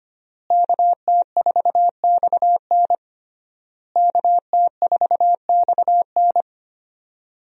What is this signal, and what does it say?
Morse code, heard as a keyed tone, about 25 wpm, 700 Hz. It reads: KT4XN KT4XN
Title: USA Call Signs - 5 Characters - Repeated 3x - 25wpm